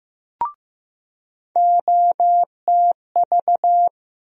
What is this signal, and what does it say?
Morse code, heard as a keyed tone, about 15 wpm, 700 Hz. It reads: OTV
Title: USA Call Sign Postfixes - 3 Letters 15wpm